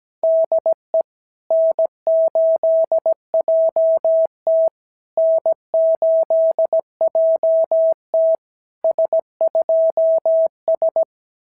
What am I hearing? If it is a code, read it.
Morse code, heard as a keyed tone, about 17 wpm, 650 Hz. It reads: DE N8JT N8JT S2S
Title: SOTA QSO Practice 17wpm